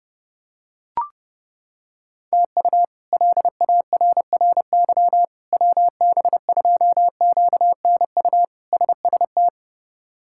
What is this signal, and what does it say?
Morse code, heard as a keyed tone, about 30 wpm, 700 Hz. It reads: TU LARRY WB2QNU SST
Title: K1USN SST QSO Practice - 30wpm